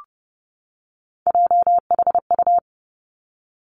Morse code, heard as a keyed tone, about 30 wpm, 700 Hz. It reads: JHU
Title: Three Challenging Letter-Number Combinations 30wpm